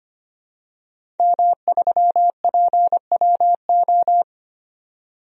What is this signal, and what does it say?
Morse code, heard as a keyed tone, about 25 wpm, 700 Hz. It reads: M3PWO
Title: DXCC-100 Call Signs - 5 Characters - 25wpm